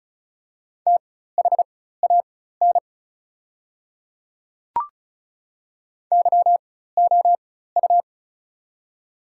Morse code, heard as a keyed tone, about 36 wpm, spaced as for 16 wpm, 690 Hz. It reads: THAN YOU
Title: 4x Character Spacing - Top 300 Words - 35wpm